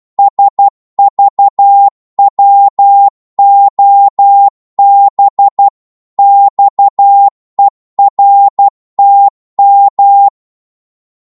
Morse code, heard as a keyed tone, about 12 wpm, 800 Hz. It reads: SVWOB XERTM
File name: Morse / ltr-12wpm